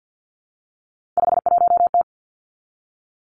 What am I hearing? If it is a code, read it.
Morse code, heard as a keyed tone, about 50 wpm, 700 Hz. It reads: 51T
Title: Three Challenging Letter-Number Combinations 50wpm